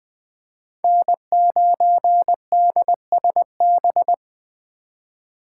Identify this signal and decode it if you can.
Morse code, heard as a keyed tone, about 21 wpm, 700 Hz. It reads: N9DSB
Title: USA Call Signs - 5 Characters - 2x Character Spacing - 20wpm